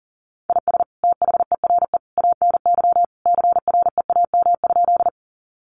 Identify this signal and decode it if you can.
Morse code, heard as a keyed tone, about 40 wpm, 700 Hz. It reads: IS THERE ANY CREAM?
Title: Sentences from Top 1500 Words 40wpm